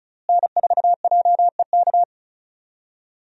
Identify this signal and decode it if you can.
Morse code, heard as a keyed tone, about 34 wpm, 700 Hz. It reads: N4JEK
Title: USA Call Signs - 5 Characters - Repeated 2x - 35wpm